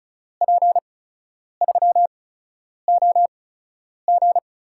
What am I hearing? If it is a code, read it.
Morse code, heard as a keyed tone, about 34 wpm, spaced as for 10 wpm, 700 Hz. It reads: P3OG
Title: USA Call Signs - 5 Characters - 8x Character Spacing - 35wpm